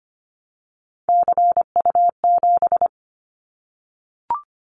Morse code, heard as a keyed tone, about 25 wpm, 700 Hz. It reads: CU7
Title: Callsign Prefixes for Top 100 DXCC - 25wpm